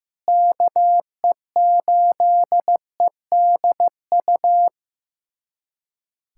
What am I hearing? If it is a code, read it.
Morse code, heard as a keyed tone, about 15 wpm, 700 Hz. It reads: KE8EDU
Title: USA Callsign DE Callsign 15wpm